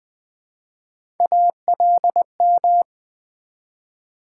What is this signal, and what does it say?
Morse code, heard as a keyed tone, about 20 wpm, 700 Hz. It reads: ALM
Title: Build A Word - 3 Letters At A Time - 6-Letter Words - 20wpm